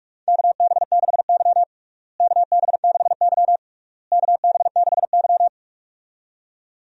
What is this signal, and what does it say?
Morse code, heard as a keyed tone, about 46 wpm, 700 Hz. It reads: KB6Y KB6Y KB6Y
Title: USA Call Signs - 4 Characters - Repeated 3x - 45wpm